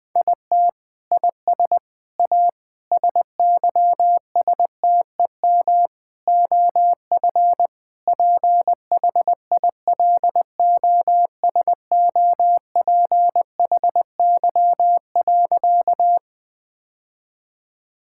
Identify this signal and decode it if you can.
Morse code, heard as a keyed tone, about 19 wpm, 700 Hz. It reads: IT IS A SYSTEM OF PHILOSOPHY.
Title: Sentences from Top 1000 Words - Review Part 6 - 20wpm